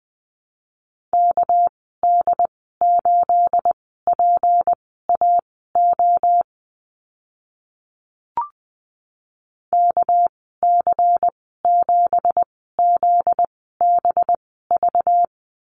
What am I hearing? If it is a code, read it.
Morse code, heard as a keyed tone, about 20 wpm, spaced as for 14 wpm, 700 Hz. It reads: KD8PAO KC7ZBV
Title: 245 Lesson 39g - Letter z - Call Signs 20wpm